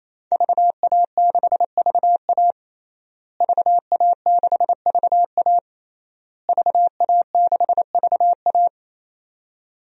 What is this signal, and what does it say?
Morse code, heard as a keyed tone, about 28 wpm, 700 Hz. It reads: VA6VA VA6VA VA6VA
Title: DXCC-100 Call Signs - 5 Characters - Repeated 3x - 28wpm